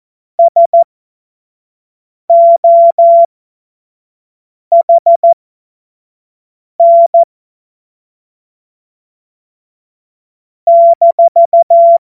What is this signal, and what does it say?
Morse code, heard as a keyed tone, about 14 wpm, spaced as for 5 wpm, 680 Hz. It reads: SOHN -
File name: Morsekurs / Witze / Witz 11. Mai 2021, 25 F70